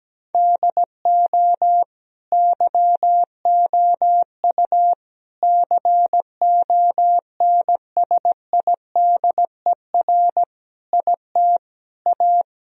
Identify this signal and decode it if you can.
Morse code, heard as a keyed tone, about 17 wpm, 700 Hz. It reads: DO YOU CONSIDER IT A
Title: Sentences from Top 1000 Words - Review Part 4 - 17wpm